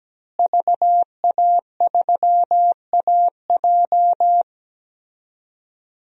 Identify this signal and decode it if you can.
Morse code, heard as a keyed tone, about 17 wpm, 700 Hz. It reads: VA3AJ
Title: DXCC-100 Call Signs - 5 Characters - Repeated 3x - 17wpm